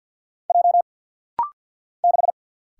Morse code, heard as a keyed tone, about 50 wpm, 700 Hz. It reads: J 6
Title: Single Letter-Number - Rapid-Fire 50wpm